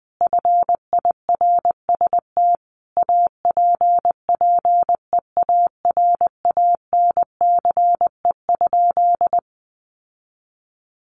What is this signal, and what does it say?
Morse code, heard as a keyed tone, about 20 wpm, 700 Hz. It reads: FIRST APPEARANCE?
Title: Sentences from Top 1100 Words 20wpm